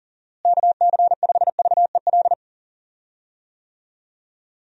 Morse code, heard as a keyed tone, about 40 wpm, 700 Hz. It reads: KC5VEL
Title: USA Call Signs - 6 Characters - Repeated 3x - 40wpm